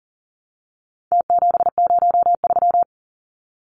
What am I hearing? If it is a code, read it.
Morse code, heard as a keyed tone, about 40 wpm, 700 Hz. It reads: T703
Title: Numbers - Contest Style - 4-Digit 40wpm